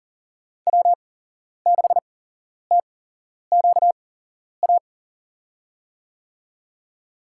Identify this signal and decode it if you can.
Morse code, heard as a keyed tone, about 37 wpm, spaced as for 11 wpm, 700 Hz. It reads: W6TQA
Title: USA Call Signs - 5 Characters - 8x Character Spacing - 40wpm